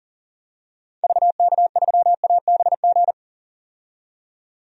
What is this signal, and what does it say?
Morse code, heard as a keyed tone, about 40 wpm, 700 Hz. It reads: VK3ABG
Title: DXCC-100 Call Signs - 6 Characters - 40wpm